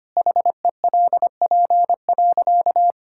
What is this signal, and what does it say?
Morse code, heard as a keyed tone, about 25 wpm, 700 Hz. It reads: HELP.
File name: practice / 25wpm / Sentences with 2-Word Phrases - 25wpm